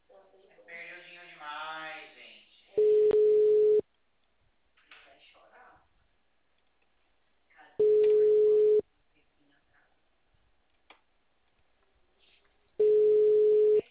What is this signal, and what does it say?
Ringback tone (Europe, 425 Hz), with speech.
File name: out- -2038- - - .46270